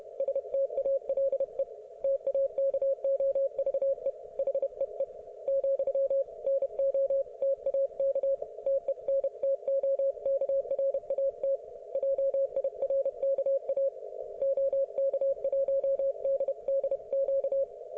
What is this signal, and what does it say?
Morse code, heard as a keyed tone, about 31 wpm, 530 Hz. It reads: STALE TAKOVE HEE , NOTAK TENTOKRAT JIRKA OK1DDQ